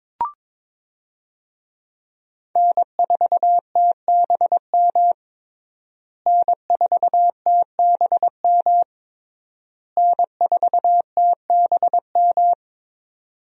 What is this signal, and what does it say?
Morse code, heard as a keyed tone, about 22 wpm, 700 Hz. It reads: N4TBM N4TBM N4TBM
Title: USA Call Signs - 5 Characters - Repeated 3x - 22wpm